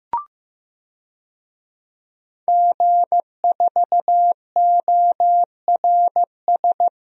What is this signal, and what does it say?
Morse code, heard as a keyed tone, about 15 wpm, 700 Hz. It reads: G4ORS
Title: DXCC-100 Call Signs - 5 Characters - 15wpm